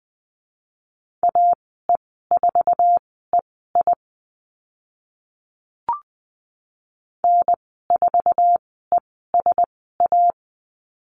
Morse code, heard as a keyed tone, about 20 wpm, spaced as for 14 wpm, 700 Hz. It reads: AE4EI N4ESA
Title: 076 Lesson 13g - Letter L - Call Signs 20wpm